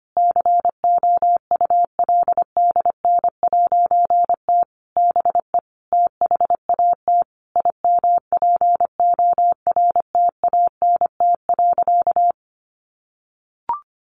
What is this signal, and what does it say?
Morse code, heard as a keyed tone, about 25 wpm, 700 Hz. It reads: COULDN'T BE THAT IMPORTANT.
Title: A Sound Of Thunder 25wpm